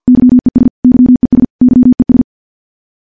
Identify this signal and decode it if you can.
Morse code, heard as a keyed tone, about 50 wpm, 250 Hz. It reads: YES YES YES
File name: morse-training-50WPM